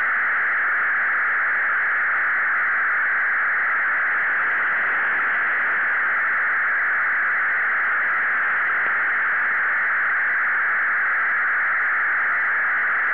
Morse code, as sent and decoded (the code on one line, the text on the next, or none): none